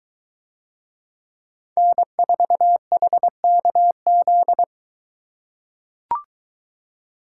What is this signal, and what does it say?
Morse code, heard as a keyed tone, about 22 wpm, 700 Hz. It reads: N4HKZ